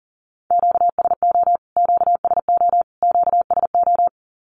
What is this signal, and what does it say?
Morse code, heard as a keyed tone, about 40 wpm, 700 Hz. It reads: QSO QSO QSO
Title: Q-Codes Repeated 3x 40wpm